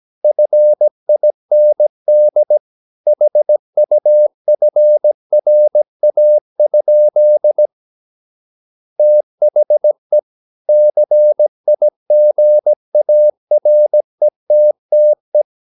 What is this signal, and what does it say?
Morse code, heard as a keyed tone, about 17 wpm, 600 Hz. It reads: FIND HUFRA? THE CIGARETTE
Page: Romanen Never af Ken Follet på engelsk.